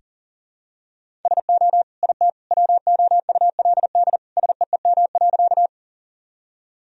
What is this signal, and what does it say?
Morse code, heard as a keyed tone, about 40 wpm, 700 Hz. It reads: SO IT WOULD SEEM.
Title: Sentences from Top 1000 Words - Review Part 1 - 40wpm